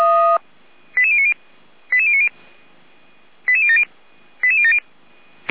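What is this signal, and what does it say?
Five-tone selective call (ZVEI), heard as quick runs of five tones, tones of about 70 ms. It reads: 88090 88090 88070 88070
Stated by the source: ZVEI five-tone selective call received via SDR, from the feuerwehr-orlamuende fire brigade audio server.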